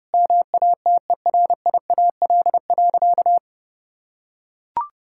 Morse code, heard as a keyed tone, about 30 wpm, 700 Hz. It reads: MATERIAL.
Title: Sentences from Top 1000 Words - Review Part 5 - 30wpm